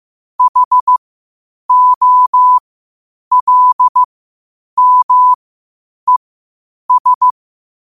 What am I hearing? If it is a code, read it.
Morse code, heard as a keyed tone, about 15 wpm, spaced as for 8 wpm, 1000 Hz. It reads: HOLMES
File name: qotdfiles / 8wpm